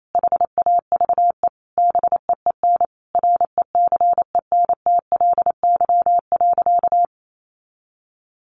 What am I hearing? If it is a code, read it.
Morse code, heard as a keyed tone, about 28 wpm, 700 Hz. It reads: HAVE BEEN RECENTLY.
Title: Sentences from Top 1200 Words 28wpm